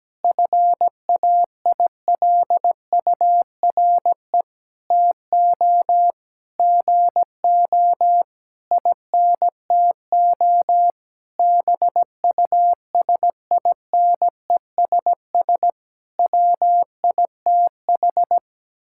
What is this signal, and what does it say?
Morse code, heard as a keyed tone, about 17 wpm, 700 Hz. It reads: FAILURE TO GO INTO BUSINESS WITH
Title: Sentences from Top 1000 Words - Review Part 6 - 17wpm